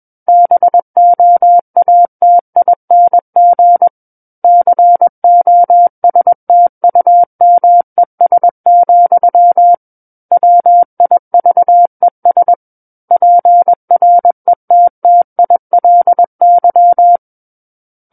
Morse code, heard as a keyed tone, about 21 wpm, 700 Hz. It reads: BOATING COSTUMES, WIVES PRETTILY
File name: Book One / 12-Chapter 12 - 21 WPM